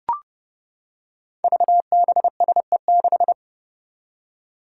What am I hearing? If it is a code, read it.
Morse code, heard as a keyed tone, about 32 wpm, 700 Hz. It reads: VBSE6
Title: Five Challenging Letter-Number Combinations 30wpm